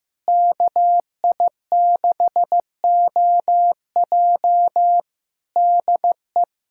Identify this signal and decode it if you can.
Morse code, heard as a keyed tone, about 15 wpm, 700 Hz. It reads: KI6OJ DE
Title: K1USN SST QSO Practice 15wpm